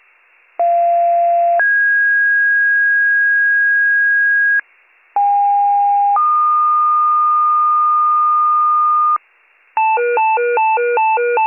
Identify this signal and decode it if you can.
Two-tone page, heard as a one-second tone, then a three-second tone, about 690 then 1650 Hz, about 810 then 1220 Hz, with noise. It is heard from 0.6 to 9.2 s.